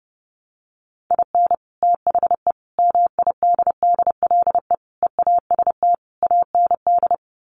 Morse code, heard as a keyed tone, about 30 wpm, 700 Hz. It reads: IN THE MIDDLE EAST AND
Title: Sets of 5 Words Encore 30wpm